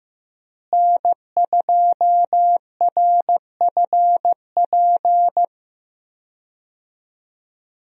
Morse code, heard as a keyed tone, about 15 wpm, 700 Hz. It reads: N2RFP
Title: USA Call Signs - 5 Characters - Repeated 3x - 15wpm